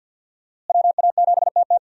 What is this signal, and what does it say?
Morse code, heard as a keyed tone, about 47 wpm, 700 Hz. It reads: WA7TT
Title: USA Call Signs - 5 Characters - Repeated 3x - 50wpm